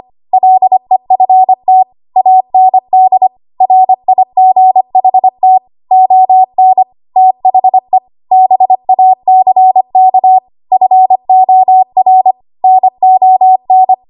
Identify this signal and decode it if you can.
Morse code, heard as a keyed tone, about 25 wpm, 750 Hz. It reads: LEFT AND RIGHT ON THE BACK FOR NON
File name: Archive / 25 WPM / 25WPM